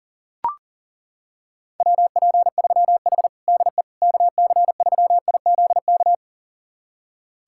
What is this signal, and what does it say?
Morse code, heard as a keyed tone, about 40 wpm, 700 Hz. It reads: WP3H DE KC3IZK
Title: USA Callsign DE Callsign 40wpm